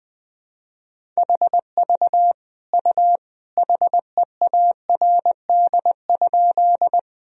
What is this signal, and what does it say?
Morse code, heard as a keyed tone, about 21 wpm, 700 Hz. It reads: HV U HEARD?
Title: Sentences from Top 400 Words and Abbreviations 20wpm